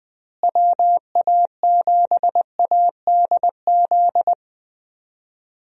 Morse code, heard as a keyed tone, about 21 wpm, 700 Hz. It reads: WA7ADZ
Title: USA Call Signs - 6 Characters - Repeated 2x - 20wpm